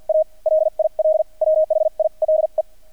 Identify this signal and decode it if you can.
Morse code, heard as a keyed tone, about 48 wpm, spaced as for 40 wpm, 630 Hz. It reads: A LIL WHILE
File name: archived / CRSnet